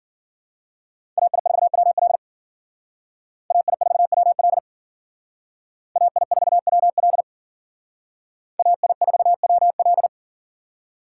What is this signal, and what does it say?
Morse code, heard as a keyed tone, about 51 wpm, 700 Hz. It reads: AI4WL AI4WL AI4WL AI4WL